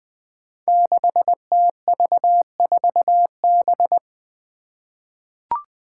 Morse code, heard as a keyed tone, about 20 wpm, 700 Hz. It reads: 6TV4B
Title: Five Challenging Letter-Number Combinations 20wpm